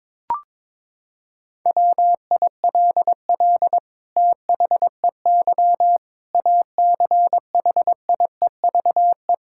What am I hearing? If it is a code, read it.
Morse code, heard as a keyed tone, about 22 wpm, 700 Hz. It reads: WILL THEY ACHIEVE
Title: Sentences from Top 1000 Words - Review Part 6 - 22wpm